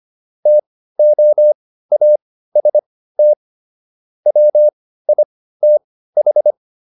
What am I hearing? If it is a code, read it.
Morse code, heard as a keyed tone, about 25 wpm, spaced as for 15 wpm, 600 Hz. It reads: TOAST WITH